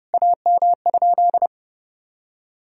Morse code, heard as a keyed tone, about 30 wpm, 700 Hz. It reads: AM?
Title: Sentences from Top 1000 Words - Review Part 2 - 30wpm